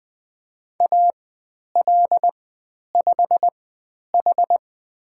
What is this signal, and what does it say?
Morse code, heard as a keyed tone, about 20 wpm, spaced as for 10 wpm, 700 Hz. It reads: AL5H
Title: 095 Lesson 16g - Letter c - Call Signs 20wpm